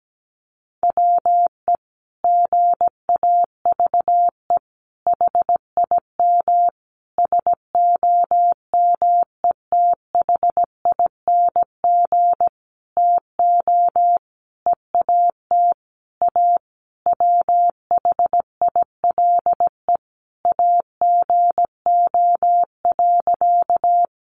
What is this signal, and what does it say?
Morse code, heard as a keyed tone, about 17 wpm, 700 Hz. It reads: WE GAVE HIM SOMETHING TO EAT A WHILE AGO.
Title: Sentences from Top 600 Words 17wpm